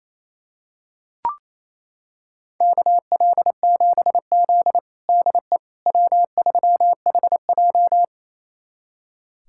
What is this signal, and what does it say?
Morse code, heard as a keyed tone, about 28 wpm, 700 Hz. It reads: KL7Z DE W3HJ
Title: USA Callsign DE Callsign - 4 Characters - 28wpm